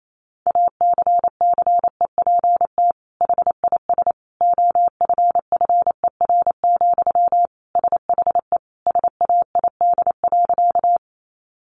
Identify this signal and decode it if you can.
Morse code, heard as a keyed tone, about 28 wpm, 700 Hz. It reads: ACCEPT HIS OFFER, SHE SAID.
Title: Sentences from Top 1300 Words 28wpm